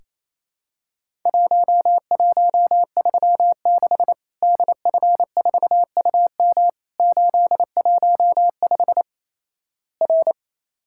Morse code, heard as a keyed tone, about 27 wpm, 700 Hz. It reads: 1136 DF4UM 815 R
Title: Worked All Europe DX Contest - QTC Exchange - 28wpm